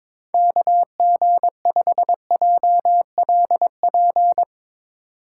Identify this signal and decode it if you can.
Morse code, heard as a keyed tone, about 21 wpm, 700 Hz. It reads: KG5JLP
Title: USA Call Signs - 6 Characters - 3x Character Spacing - 22wpm